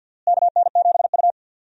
Morse code, heard as a keyed tone, about 52 wpm, 700 Hz. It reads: KN7U
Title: USA Call Signs - 4 Characters - Repeated 3x - 50wpm